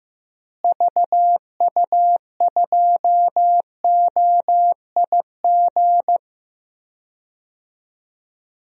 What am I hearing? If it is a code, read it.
Morse code, heard as a keyed tone, about 15 wpm, 700 Hz. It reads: VU2OIG
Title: DXCC-100 Call Signs - 6 Characters - Repeated 2x - 15wpm